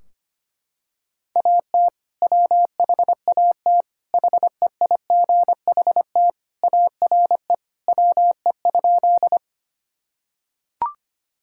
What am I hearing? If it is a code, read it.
Morse code, heard as a keyed tone, about 25 wpm, 700 Hz. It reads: AT WHAT HEIGHT ARE WE?
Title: Sentences from Top 1600 Words 25wpm